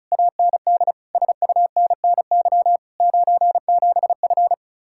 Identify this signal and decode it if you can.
Morse code, heard as a keyed tone, about 43 wpm, 700 Hz. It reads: AND SUNNY 97F
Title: Speed-Racing Common QSO Phrases - 35wpm